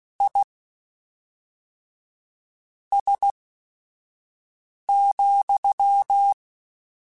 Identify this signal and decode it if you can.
Morse code, heard as a keyed tone, about 16 wpm, spaced as for 4 wpm, 800 Hz. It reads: IS,
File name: General / General-mp3s / Suffixes sp @5